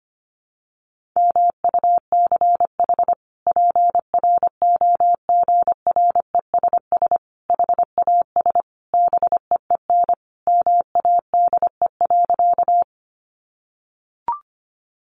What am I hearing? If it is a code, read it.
Morse code, heard as a keyed tone, about 25 wpm, 700 Hz. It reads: MUCH PROGRESS HAS BEEN MADE.
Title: Sentences from Top 1400 Words 25wpm